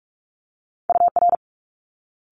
Morse code, heard as a keyed tone, about 46 wpm, 700 Hz. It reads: UR
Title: Multi-letter Phonemes Quick Fire 45wpm